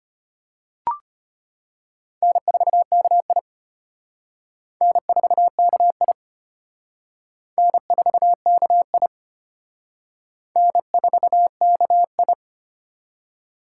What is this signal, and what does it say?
Morse code, heard as a keyed tone, about 32 wpm, 700 Hz. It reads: N4KI N4KI N4KI N4KI